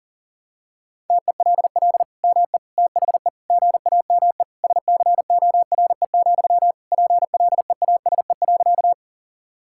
Morse code, heard as a keyed tone, about 40 wpm, 700 Hz. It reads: TELL ME THE GAME SCORE, PLEASE.
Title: Sentences from Top 1600 Words 40wpm